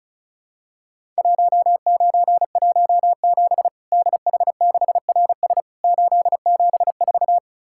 Morse code, heard as a keyed tone, about 34 wpm, 700 Hz. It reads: 1917 DH6RS 874
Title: Worked All Europe DX Contest - QTC Exchange - 35wpm